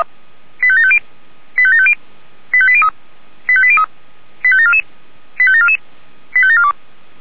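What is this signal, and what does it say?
Five-tone selective call (ZVEI), heard as quick runs of five tones, tones of about 70 ms. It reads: 85750 85750 85803 85803 85744 85744 85742
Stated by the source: ZVEI five-tone selective call, address 85730, received by SDR.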